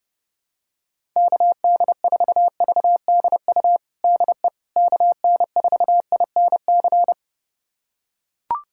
Morse code, heard as a keyed tone, about 30 wpm, 700 Hz. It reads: KD4VDU DE KN4INC
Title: USA Callsign DE Callsign - 6 Characters - 30wpm